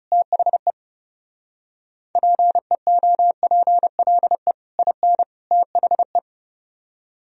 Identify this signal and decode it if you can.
Morse code, heard as a keyed tone, about 30 wpm, 700 Hz. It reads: THE PEOPLE IN THE
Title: Speed-Racing - Sets of 3 Words 30wpm